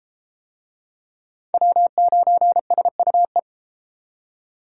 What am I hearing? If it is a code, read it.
Morse code, heard as a keyed tone, about 31 wpm, 700 Hz. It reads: W9SUE